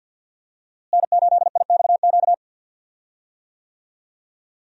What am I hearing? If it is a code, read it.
Morse code, heard as a keyed tone, about 50 wpm, 700 Hz. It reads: N8IXQ
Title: USA Call Signs - 5 Characters - Repeated 2x - 50wpm